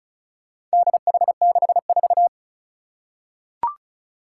Morse code, heard as a keyed tone, about 36 wpm, 700 Hz. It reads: DH64
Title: Four Challenging Letter-Number Combinations 35wpm